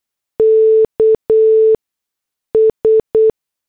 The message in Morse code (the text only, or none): KS